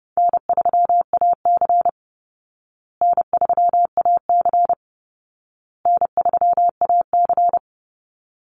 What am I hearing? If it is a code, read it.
Morse code, heard as a keyed tone, about 30 wpm, 700 Hz. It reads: N3AC N3AC N3AC
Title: Easy Call Signs Repeated 3x - 30wpm